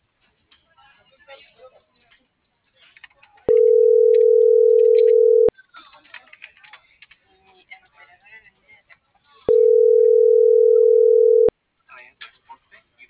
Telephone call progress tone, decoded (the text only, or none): ringback tone